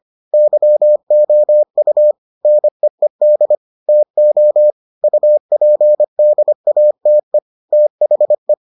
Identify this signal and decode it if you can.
Morse code, heard as a keyed tone, about 25 wpm, 600 Hz. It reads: YOU NEED TO UPDATE THE